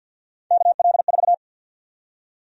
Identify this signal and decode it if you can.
Morse code, heard as a keyed tone, about 51 wpm, 700 Hz. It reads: KL4
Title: USA Call Sign Prefixes - Repeated 2x - 50wpm